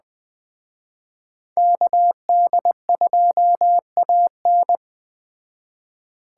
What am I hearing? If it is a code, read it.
Morse code, heard as a keyed tone, about 20 wpm, 700 Hz. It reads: KD2AN
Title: Speed Racing - USA 2x2 Call Signs 17wpm